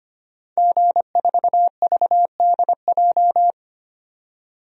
Morse code, heard as a keyed tone, about 25 wpm, 700 Hz. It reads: G4VDJ
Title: DXCC-100 Call Signs - 5 Characters - 25wpm